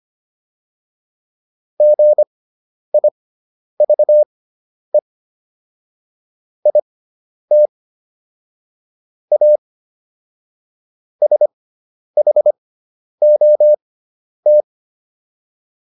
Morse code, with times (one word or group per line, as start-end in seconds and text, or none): 1.8-5.0 GIVE
6.7-7.7 IT
9.3-9.6 A
11.2-14.6 SHOT